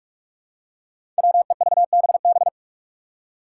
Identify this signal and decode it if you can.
Morse code, heard as a keyed tone, about 46 wpm, 700 Hz. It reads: WEVBB
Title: Five Challenging Letter-Number Combinations 45wpm